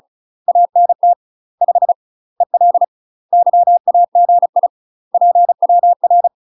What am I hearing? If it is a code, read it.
Morse code, heard as a keyed tone, about 34 wpm, 700 Hz. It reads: ANT 5 EL YAGI PWR
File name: UK MASTER QSOs TRAINING/35WPM